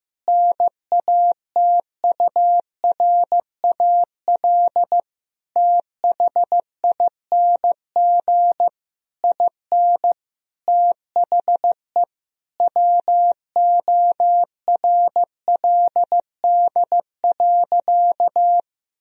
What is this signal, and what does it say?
Morse code, heard as a keyed tone, about 15 wpm, 700 Hz. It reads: NATURAL THING IN THE WORLD.
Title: Sentences from Top 1000 Words - Review Part 4 - 15wpm